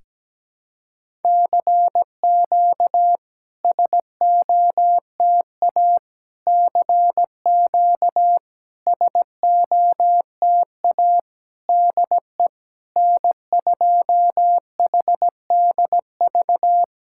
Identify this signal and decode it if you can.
Morse code, heard as a keyed tone, about 17 wpm, 700 Hz. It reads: CQ SOTA CQ SOTA DE N2HDV
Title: SOTA QSO Practice 17wpm